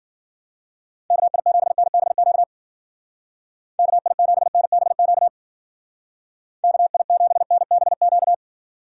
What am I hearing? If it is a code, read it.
Morse code, heard as a keyed tone, about 50 wpm, 700 Hz. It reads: KI7NBQ KI7NBQ KI7NBQ